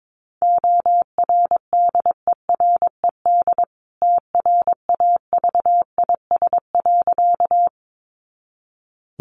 Morse code, heard as a keyed tone, about 22 wpm, 700 Hz. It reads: ORDERED TRAVIS.
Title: A Sound Of Thunder 22wpm